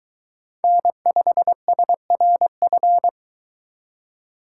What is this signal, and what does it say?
Morse code, heard as a keyed tone, about 22 wpm, 700 Hz. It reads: N5SRF